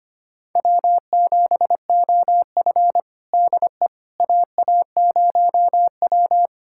Morse code, heard as a keyed tone, about 25 wpm, 700 Hz. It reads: W7OF DE AA0W
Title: USA Callsign DE Callsign - 4 Characters - 25wpm